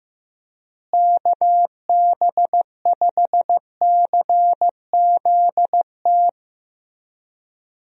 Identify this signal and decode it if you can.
Morse code, heard as a keyed tone, about 15 wpm, 700 Hz. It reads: KB5CZT
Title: USA Call Signs - 6 Characters - Repeated 3x - 15wpm